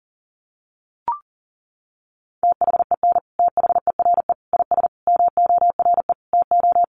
Morse code, heard as a keyed tone, about 40 wpm, 700 Hz. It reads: THEN THERE IS MORE TO
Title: Sentences from Top 100 Words 40wpm